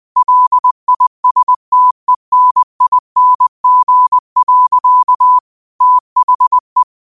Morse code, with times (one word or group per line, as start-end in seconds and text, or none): 0.2-5.4 LISTENING.
5.8-6.8 THE